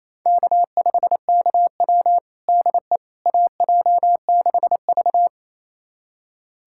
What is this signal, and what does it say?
Morse code, heard as a keyed tone, about 28 wpm, 700 Hz. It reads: K5KW DE AJ6V
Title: USA Callsign DE Callsign - 4 Characters - 28wpm